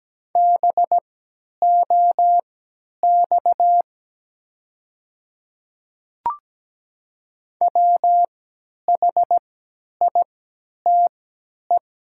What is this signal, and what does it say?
Morse code, heard as a keyed tone, about 17 wpm, spaced as for 10 wpm, 700 Hz. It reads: BOX WHITE
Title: 3x Character Spacing - Top 500 Words - 17wpm